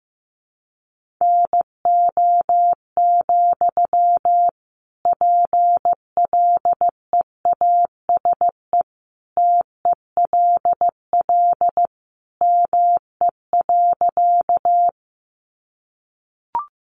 Morse code, heard as a keyed tone, about 15 wpm, 700 Hz. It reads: NO, PLEASE TELL ME.
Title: Sentences from Top 1500 Words 15wpm